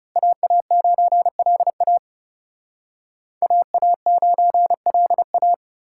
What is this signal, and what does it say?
Morse code, heard as a keyed tone, about 38 wpm, 700 Hz. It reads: AA9LA AA9LA